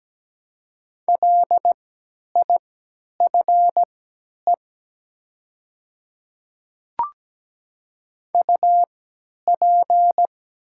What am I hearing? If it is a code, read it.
Morse code, heard as a keyed tone, about 17 wpm, spaced as for 10 wpm, 700 Hz. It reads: LIFE UP
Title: 3x Character Spacing - Top 300 Words - 17wpm